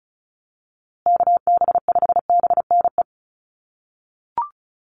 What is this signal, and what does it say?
Morse code, heard as a keyed tone, about 34 wpm, 700 Hz. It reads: KB5BNE
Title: USA Call Signs Repeated 3x 35wpm